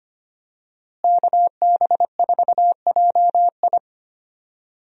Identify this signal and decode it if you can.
Morse code, heard as a keyed tone, about 25 wpm, 700 Hz. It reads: KB4JI
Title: USA Call Signs - 5 Characters - Repeated 2x - 25wpm